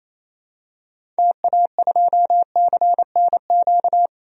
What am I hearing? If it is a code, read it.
Morse code, heard as a keyed tone, about 28 wpm, 700 Hz. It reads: TA2CNQ
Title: DXCC-100 Call Signs - 6 Characters - 28wpm